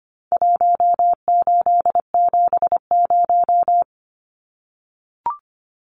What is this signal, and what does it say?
Morse code, heard as a keyed tone, about 25 wpm, 700 Hz. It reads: 1870
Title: Numbers - Contest Style - 4-Digit 25wpm